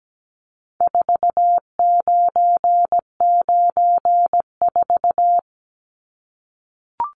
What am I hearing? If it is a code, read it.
Morse code, heard as a keyed tone, about 17 wpm, 700 Hz. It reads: 4994
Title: Numbers - 4-Digit 17wpm